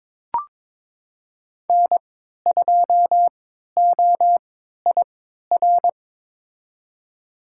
Morse code, heard as a keyed tone, about 22 wpm, spaced as for 12 wpm, 700 Hz. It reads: N2OIR
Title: USA Call Signs - 5 Characters - 3x Character Spacing - 22wpm